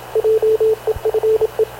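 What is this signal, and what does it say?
Morse code, heard as a keyed tone, about 27 wpm, 440 Hz, with radio noise. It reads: JEFE
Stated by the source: Havana Radio CLA press (PX) broadcast, recorded by Radio Officer Paul Dumesnil aboard mv CNAV QUEST/CZDO.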